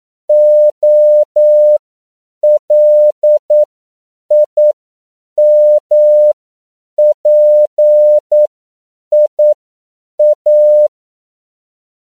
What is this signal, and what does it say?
Morse code, heard as a keyed tone, about 9 wpm, spaced as for 7 wpm, 600 Hz. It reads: OLIMPIA